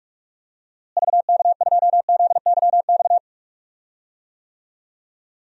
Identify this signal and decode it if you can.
Morse code, heard as a keyed tone, about 46 wpm, 700 Hz. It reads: VK2ZYX
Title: DXCC-100 Call Signs - 6 Characters - Repeated 3x - 45wpm